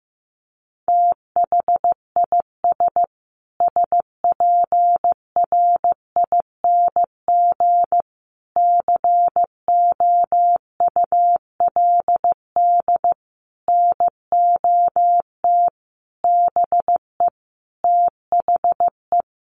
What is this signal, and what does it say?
Morse code, heard as a keyed tone, about 15 wpm, 700 Hz. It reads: THIS SPRING COULD NOT BE THE
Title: Sentences from Top 1500 Words 15wpm